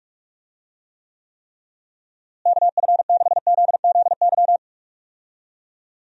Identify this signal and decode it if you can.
Morse code, heard as a keyed tone, about 42 wpm, 700 Hz. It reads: KF6ZZY
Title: USA Call Signs - 6 Characters - Repeated 3x - 45wpm